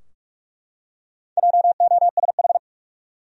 Morse code, heard as a keyed tone, about 43 wpm, 700 Hz. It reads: JOSH
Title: USA Common Names - Max 4 Letters - 45wpm